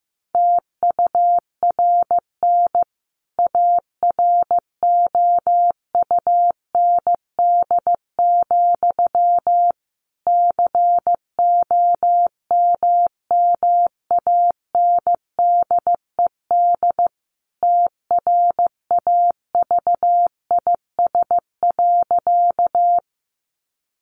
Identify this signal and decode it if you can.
Morse code, heard as a keyed tone, about 15 wpm, 700 Hz. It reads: TURN AROUND, COMMANDED TRAVIS.
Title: A Sound Of Thunder 15wpm